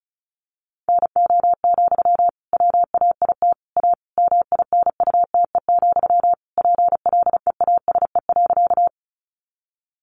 Morse code, heard as a keyed tone, about 34 wpm, 700 Hz. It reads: NO, WAIT A MINUTE, PLEASE.
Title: Sentences from Top 1500 Words 35wpm